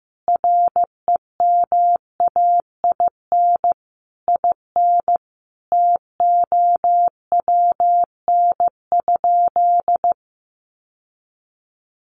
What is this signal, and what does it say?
Morse code, heard as a keyed tone, about 15 wpm, 700 Hz. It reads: REMAIN IN TOWN?
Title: Sentences from Top 1200 Words 15wpm